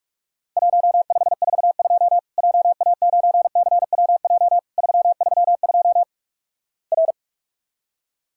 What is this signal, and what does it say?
Morse code, heard as a keyed tone, about 46 wpm, 700 Hz. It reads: 1542 JA9CWJ 332 R
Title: Worked All Europe DX Contest - QTC Exchange - 45wpm